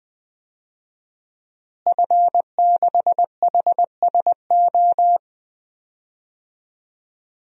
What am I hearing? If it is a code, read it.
Morse code, heard as a keyed tone, about 20 wpm, 700 Hz. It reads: F6HSO
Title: DXCC-100 Call Signs - 5 Characters - Repeated 3x - 20wpm